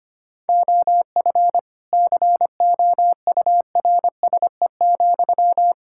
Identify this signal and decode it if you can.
Morse code, heard as a keyed tone, about 25 wpm, 700 Hz. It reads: OF COURSE,
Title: Sentences from Top 1000 Words - Review Part 5 - 25wpm